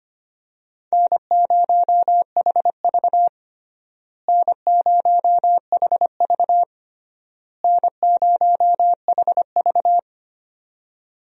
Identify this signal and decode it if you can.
Morse code, heard as a keyed tone, about 25 wpm, 700 Hz. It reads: N0HV N0HV N0HV
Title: USA Call Signs - 4 Characters - Repeated 3x - 25wpm